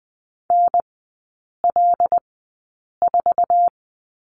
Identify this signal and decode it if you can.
Morse code, heard as a keyed tone, about 20 wpm, spaced as for 8 wpm, 700 Hz. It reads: NL4
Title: 074 Lesson 13e - Letter L - Three Characters 20wpm